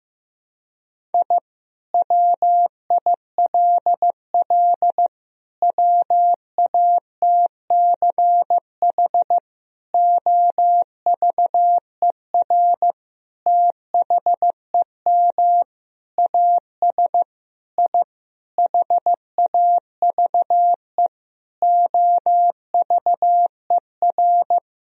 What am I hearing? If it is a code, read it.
Morse code, heard as a keyed tone, about 15 wpm, 700 Hz. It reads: I WILL WATCH OVER THEM AS I HAVE OVER
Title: Sentences from Top 1000 Words - Review Part 1 - 15wpm